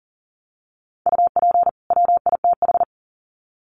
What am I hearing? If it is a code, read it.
Morse code, heard as a keyed tone, about 40 wpm, 700 Hz. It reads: UP WITH
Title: Sets of 2 Words 40wpm